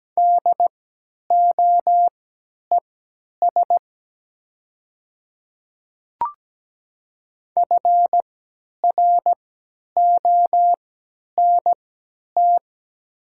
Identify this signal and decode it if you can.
Morse code, heard as a keyed tone, about 17 wpm, spaced as for 10 wpm, 700 Hz. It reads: DOES FRONT
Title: 3x Character Spacing - Top 500 Words - 17wpm